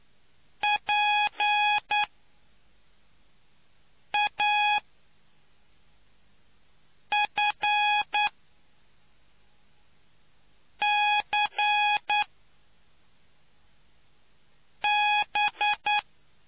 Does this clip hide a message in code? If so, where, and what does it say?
Yes: Morse code from 0.6 to 16.0 s, heard as a keyed tone, about 9 wpm, spaced as for 3 wpm, 830 Hz. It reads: PAFCB